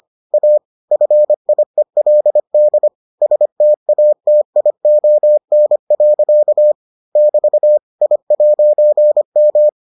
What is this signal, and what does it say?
Morse code, heard as a keyed tone, about 25 wpm, 600 Hz. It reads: A FIELD STATION. <BT> I'M